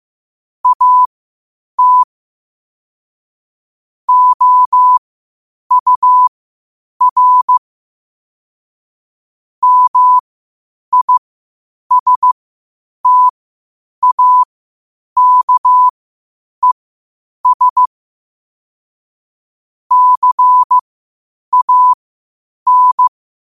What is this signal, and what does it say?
Morse code, heard as a keyed tone, about 15 wpm, spaced as for 8 wpm, 1000 Hz. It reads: AT OUR MISTAKES CAN
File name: qotdfiles / 8wpm